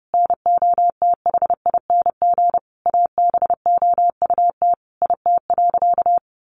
Morse code, heard as a keyed tone, about 30 wpm, 700 Hz. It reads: NOTHING ABOUT IT.
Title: Sentences from Top 500 Words 30wpm